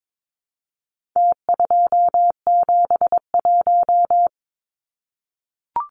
Morse code, heard as a keyed tone, about 22 wpm, 700 Hz. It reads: T271
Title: Numbers - Contest Style - 4-Digit 22wpm